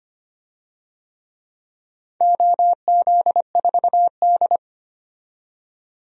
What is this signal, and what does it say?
Morse code, heard as a keyed tone, about 25 wpm, 700 Hz. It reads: OZ4D
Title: DXCC-100 Call Signs - 4 Characters - 25wpm